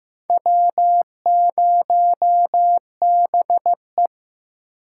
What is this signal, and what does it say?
Morse code, heard as a keyed tone, about 15 wpm, 700 Hz. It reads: W0BE
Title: USA Call Signs - 4 Characters - 3x Character Spacing - 15wpm